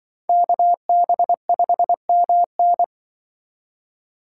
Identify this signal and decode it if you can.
Morse code, heard as a keyed tone, about 24 wpm, 700 Hz. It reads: KB5MN
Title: Speed Racing - USA 2x2 Call Signs 20wpm